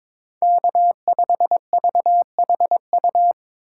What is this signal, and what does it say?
Morse code, heard as a keyed tone, about 22 wpm, 700 Hz. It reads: K5VHU
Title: USA Call Signs - 5 Characters - 2x Character Spacing - 22wpm